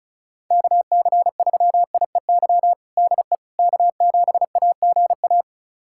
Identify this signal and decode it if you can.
Morse code, heard as a keyed tone, about 36 wpm, 700 Hz. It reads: KC3IEY DE K7AGA
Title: USA Callsign DE Callsign 35wpm